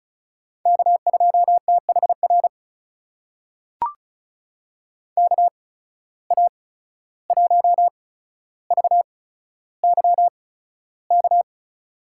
Morse code, heard as a keyed tone, about 36 wpm, spaced as for 14 wpm, 700 Hz. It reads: K2THR KA1VYK
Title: USA Call Signs - 8x Character Spacing - 35wpm